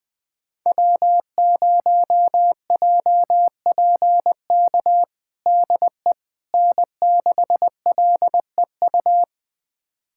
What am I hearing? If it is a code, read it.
Morse code, heard as a keyed tone, about 20 wpm, 700 Hz. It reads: W0JPK DE N6LEU
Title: USA Callsign DE Callsign - 5 Characters - 20wpm